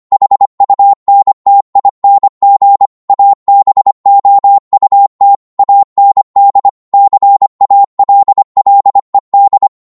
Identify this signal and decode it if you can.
Morse code, heard as a keyed tone, about 25 wpm, 800 Hz. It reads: HUNTING ABOUT AND CALLED
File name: 25wpm